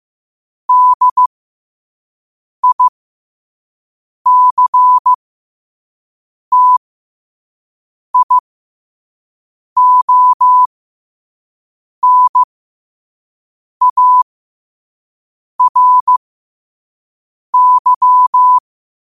Morse code, heard as a keyed tone, about 15 wpm, spaced as for 5 wpm, 1000 Hz. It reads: DICTIONARY
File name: qotdfiles / 5wpm